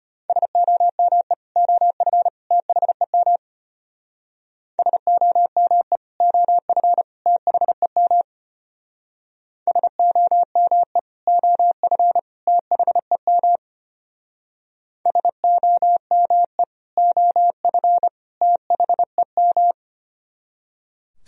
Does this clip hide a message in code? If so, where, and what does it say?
Yes: Morse code from 0.3 to 19.7 s, heard as a keyed tone, about 31 wpm, 700 Hz. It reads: SOME OF THEM SOME OF THEM SOME OF THEM SOME OF THEM